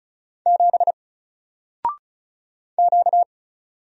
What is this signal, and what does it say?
Morse code, heard as a keyed tone, about 34 wpm, 700 Hz. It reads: 7 Q
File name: Single-Letter-Number-35wpm